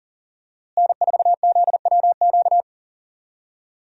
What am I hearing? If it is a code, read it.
Morse code, heard as a keyed tone, about 40 wpm, 700 Hz. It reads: N4ZWQ
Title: USA Call Signs - 5 Characters - Repeated 3x - 40wpm